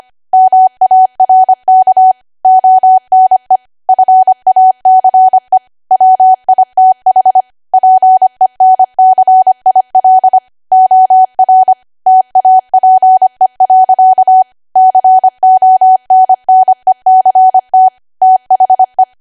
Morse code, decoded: MARK ONE FACE WITH PENCIL OR TAPE. CONNECT THE